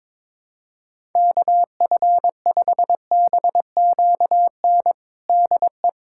Morse code, heard as a keyed tone, about 22 wpm, 700 Hz. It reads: KF5BQN DE
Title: USA Callsign DE Callsign - 6 Characters - 22wpm